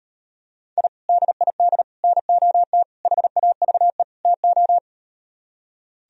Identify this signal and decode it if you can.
Morse code, heard as a keyed tone, about 38 wpm, 700 Hz. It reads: I DID NOT HAVE TO